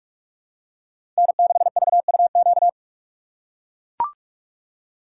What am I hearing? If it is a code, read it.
Morse code, heard as a keyed tone, about 42 wpm, 690 Hz. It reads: N6VUQ
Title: USA Call Signs - 5 Characters - 8x Character Spacing - 45wpm